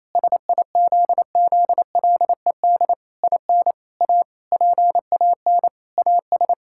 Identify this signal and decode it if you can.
Morse code, heard as a keyed tone, about 28 wpm, 700 Hz. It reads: SIZZLED IN A PAN AS
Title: Pangram Challenge v2 - 28wpm